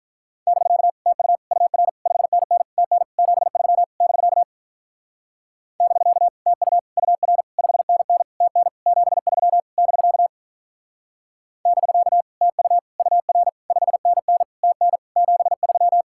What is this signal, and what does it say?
Morse code, heard as a keyed tone, about 44 wpm, 700 Hz. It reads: <BK> TU UR 5NN TN 73 <BK> <BK> TU UR 5NN TN 73 <BK> <BK> TU UR 5NN TN 73